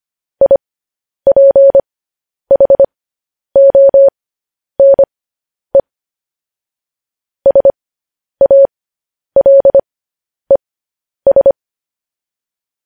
Morse code, heard as a keyed tone, about 25 wpm, spaced as for 10 wpm, 550 Hz. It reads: IPHONE SALES